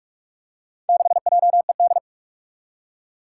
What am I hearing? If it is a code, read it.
Morse code, heard as a keyed tone, about 44 wpm, 700 Hz. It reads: 6JED
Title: Four Challenging Letter-Number Combinations 45wpm